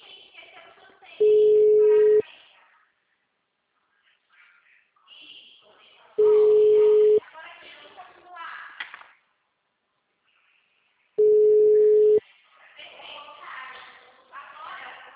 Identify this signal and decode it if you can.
Ringback tone (Europe, 425 Hz), with speech.